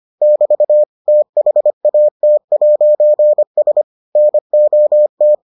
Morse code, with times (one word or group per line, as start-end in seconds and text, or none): 0.2-0.8 <BT>
1.1-3.8 THAT'S
4.2-5.3 NOT